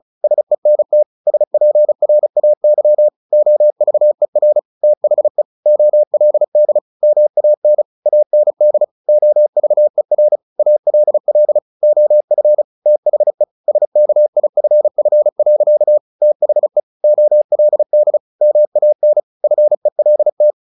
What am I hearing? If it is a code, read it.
Morse code, heard as a keyed tone, about 34 wpm, 600 Hz. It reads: SENT SPRAY OVER THE OLD MAN AND OVER ALL OF THE SKIFF. THE OLD MAN FELT
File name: the old man and the sea 2 35wpm